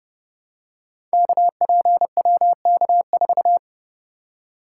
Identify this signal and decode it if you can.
Morse code, heard as a keyed tone, about 30 wpm, 700 Hz. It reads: KPWK4
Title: Code Groups - 5 Letter:Numbers - 30wpm